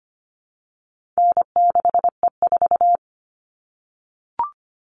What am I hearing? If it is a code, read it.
Morse code, heard as a keyed tone, about 25 wpm, 700 Hz. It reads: N6E4
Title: All Cut Numbers - 4-Digit 25wpm